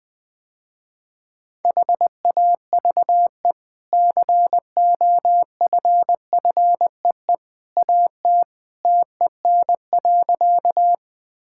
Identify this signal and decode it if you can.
Morse code, heard as a keyed tone, about 20 wpm, 700 Hz. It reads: HAVE COFFEE AT TEN.
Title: Easy Sentences with Hints - 20wpm